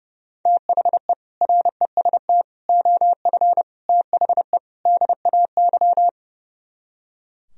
Morse code, heard as a keyed tone, about 36 wpm, 700 Hz. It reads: THE REST OF THE DAY
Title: Speed-Racing - Sets of 5 Words 30wpm